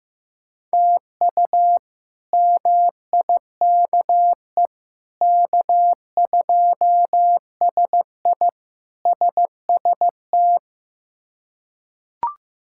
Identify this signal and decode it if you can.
Morse code, heard as a keyed tone, about 16 wpm, 700 Hz. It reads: TU MIKE K2SI SST
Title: K1USN SST QSO Practice - 15wpm